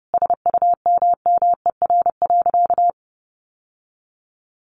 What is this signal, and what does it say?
Morse code, heard as a keyed tone, about 30 wpm, 700 Hz. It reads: SUMMER.
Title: Sentences from Top 1100 Words 30wpm